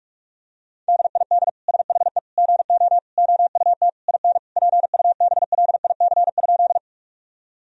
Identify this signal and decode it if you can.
Morse code, heard as a keyed tone, about 41 wpm, 700 Hz. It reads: DID SHE GO OUT IN PUBLIC?
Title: Sentences from Top 1000 Words - Review Part 3 - 45wpm